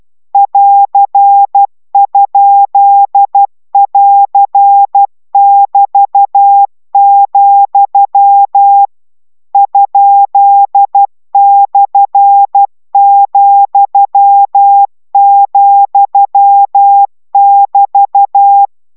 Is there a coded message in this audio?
Yes: Morse code from 0.3 to 18.7 s, heard as a keyed tone, about 12 wpm, 800 Hz. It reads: <AR>?<AR><BT>, ?/,,<BT>